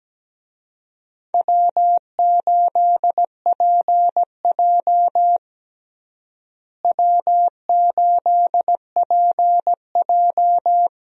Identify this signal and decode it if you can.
Morse code, heard as a keyed tone, about 17 wpm, 700 Hz. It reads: W8PJ W8PJ
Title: USA Call Signs - 4 Characters - Repeated 3x - 17wpm